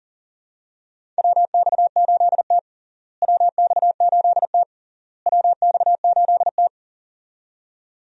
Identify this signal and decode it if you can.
Morse code, heard as a keyed tone, about 37 wpm, 700 Hz. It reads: WX8T WX8T WX8T
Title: USA Call Signs - 4 Characters - Repeated 3x - 40wpm